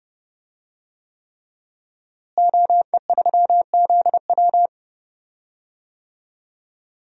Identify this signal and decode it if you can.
Morse code, heard as a keyed tone, about 30 wpm, 700 Hz. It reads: OE3ZW
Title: DXCC-100 Call Signs - 5 Characters - Repeated 3x - 30wpm